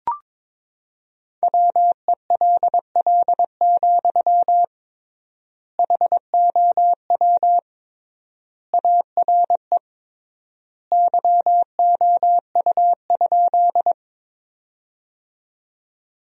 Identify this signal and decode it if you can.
Morse code, heard as a keyed tone, about 22 wpm, 700 Hz. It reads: WELL, HOW ARE YOU?
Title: 3x Word-Spacing - Sentences from Top 200 Words - 22wpm